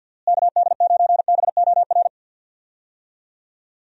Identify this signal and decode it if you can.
Morse code, heard as a keyed tone, about 50 wpm, 700 Hz. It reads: KD9BOR
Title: USA Call Signs - 6 Characters - Repeated 3x - 50wpm